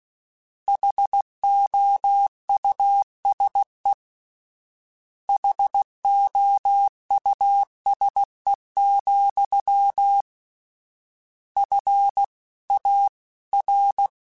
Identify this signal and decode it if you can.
Morse code, heard as a keyed tone, about 16 wpm, 800 Hz. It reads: HOUSE HOUSE, F A R
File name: General / General-mp3s / Houses @15